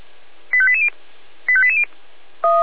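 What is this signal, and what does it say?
Five-tone selective call (ZVEI), heard as quick runs of five tones, tones of about 70 ms. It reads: 85880 85880
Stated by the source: ZVEI five-tone call received by SDR.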